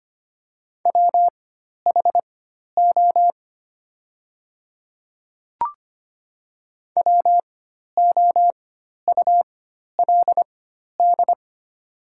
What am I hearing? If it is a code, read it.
Morse code, heard as a keyed tone, about 25 wpm, spaced as for 12 wpm, 700 Hz. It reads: WHO WOULD
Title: 4x Character Spacing - Top 100 Words - 25wpm